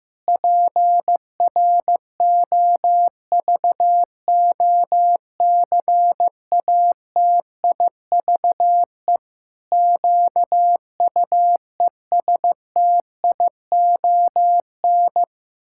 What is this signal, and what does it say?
Morse code, heard as a keyed tone, about 15 wpm, 700 Hz. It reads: PROVOCATIVE QUESTION
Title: Pangram Challenge v2 - 15wpm